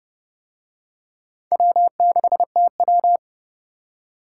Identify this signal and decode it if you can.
Morse code, heard as a keyed tone, about 29 wpm, 700 Hz. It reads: W6TW